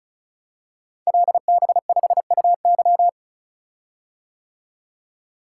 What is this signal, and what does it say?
Morse code, heard as a keyed tone, about 36 wpm, 700 Hz. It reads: LB5UY
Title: Code Groups - 5 Letter:Numbers - 35wpm